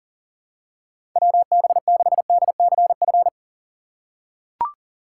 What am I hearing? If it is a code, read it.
Morse code, heard as a keyed tone, about 37 wpm, 700 Hz. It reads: WB6DCF
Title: USA Call Signs - 2x Character Spacing - 40wpm